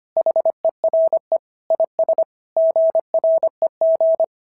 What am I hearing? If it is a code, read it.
Morse code, heard as a keyed tone, about 24 wpm, 650 Hz. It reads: HERE IS GREG
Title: SOTA QSO Practice 25wpm